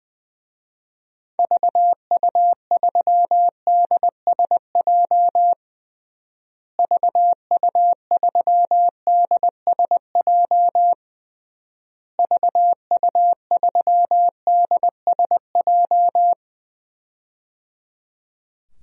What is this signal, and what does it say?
Morse code, heard as a keyed tone, about 19 wpm, 700 Hz. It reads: VU3DSJ VU3DSJ VU3DSJ
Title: DXCC-100 Call Signs - 6 Characters - Repeated 3x - 20wpm